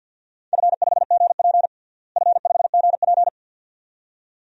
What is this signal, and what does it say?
Morse code, heard as a keyed tone, about 51 wpm, 700 Hz. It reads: F5GP F5GP
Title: DXCC-100 Call Signs - 4 Characters - Repeated 2x - 50wpm